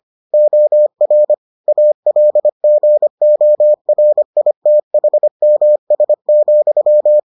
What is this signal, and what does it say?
Morse code, heard as a keyed tone, about 25 wpm, 600 Hz. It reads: OR ALGORITHMS,